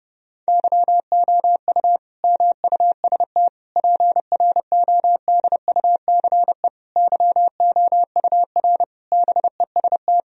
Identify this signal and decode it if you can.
Morse code, heard as a keyed tone, about 32 wpm, 700 Hz. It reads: YOU MUST PRODUCE YOUR BEST
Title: Sentences from Top 1000 Words - Review Part 3 - 30wpm